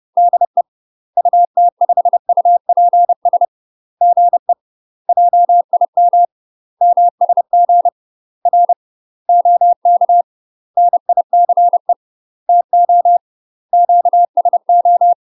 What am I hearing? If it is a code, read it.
Morse code, heard as a keyed tone, about 30 wpm, 700 Hz. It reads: DE UT5UPS GE JIM MSG R OK NICE TO QSO
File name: UK MASTER QSOs TRAINING / 30WPM / UK MASTER QSOs 0044